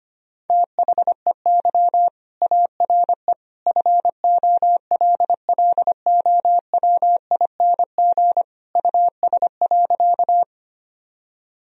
Morse code, heard as a keyed tone, about 25 wpm, 700 Hz. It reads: THEY ARE FOLLOWING US.
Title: Sentences from Top 1000 Words - Review Part 4 - 25wpm